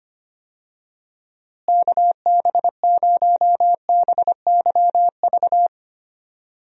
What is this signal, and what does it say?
Morse code, heard as a keyed tone, about 25 wpm, 700 Hz. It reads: KB0BYV
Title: USA Call Signs - 6 Characters - Repeated 3x - 25wpm